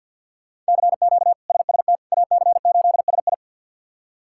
Morse code, heard as a keyed tone, about 52 wpm, 700 Hz. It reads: CQ SST AC8SI
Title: K1USN SST QSO Practice - 50wpm